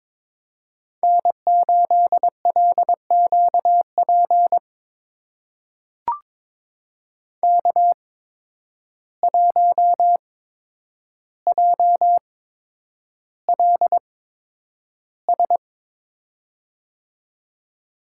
Morse code, heard as a keyed tone, about 22 wpm, spaced as for 9 wpm, 700 Hz. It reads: N8LQP K1JLS
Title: USA Call Signs - 5 Characters - 8x Character Spacing - 22wpm